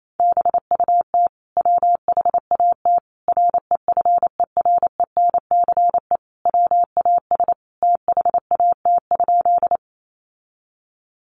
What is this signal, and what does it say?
Morse code, heard as a keyed tone, about 28 wpm, 700 Hz. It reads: BUT WHAT REFERENCE WAS THAT?
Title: Sentences from Top 1200 Words 28wpm